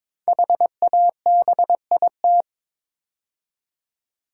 Morse code, heard as a keyed tone, about 22 wpm, 700 Hz. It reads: HABIT
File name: 5 Letter Words - First 3 Letters Given - 22wpm